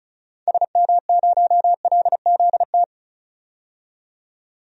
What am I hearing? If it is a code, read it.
Morse code, heard as a keyed tone, about 34 wpm, 700 Hz. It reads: SM0LZT
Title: DXCC-100 Call Signs - 6 Characters - 35wpm